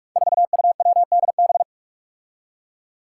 Morse code, heard as a keyed tone, about 46 wpm, 700 Hz. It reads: 4UWDB
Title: Five Challenging Letter-Number Combinations 45wpm